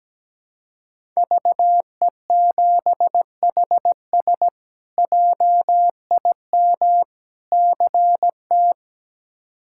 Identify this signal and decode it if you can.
Morse code, heard as a keyed tone, about 17 wpm, 700 Hz. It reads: VE7HS JIM CT
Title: K1USN SST QSO Practice - 17wpm